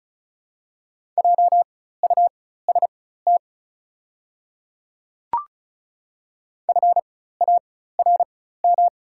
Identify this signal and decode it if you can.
Morse code, heard as a keyed tone, about 34 wpm, spaced as for 16 wpm, 700 Hz. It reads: JUST FARM
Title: 4x Character Spacing - Top 500 Words - 35wpm